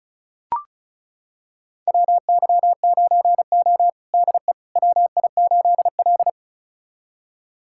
Morse code, heard as a keyed tone, about 36 wpm, 700 Hz. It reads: WY9O DE WI8L
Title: USA Callsign DE Callsign - 4 Characters - 35wpm